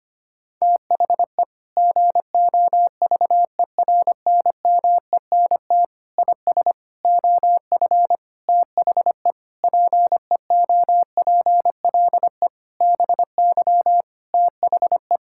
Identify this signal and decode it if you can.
Morse code, heard as a keyed tone, about 25 wpm, 700 Hz. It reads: THE GOVERNMENT IS OF THE PEOPLE BY THE
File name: practice / 25wpm / Sentences with 9-Word Phrases - 25wpm